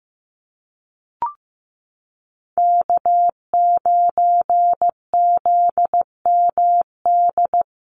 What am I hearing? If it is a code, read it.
Morse code, heard as a keyed tone, about 15 wpm, 700 Hz. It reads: K9ZMD
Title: NA SOTA Chaser Callsigns 15wpm